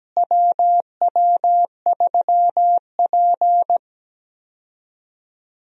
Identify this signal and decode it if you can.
Morse code, heard as a keyed tone, about 17 wpm, 700 Hz. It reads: WW3P
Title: USA Call Signs - 4 Characters - Repeated 3x - 17wpm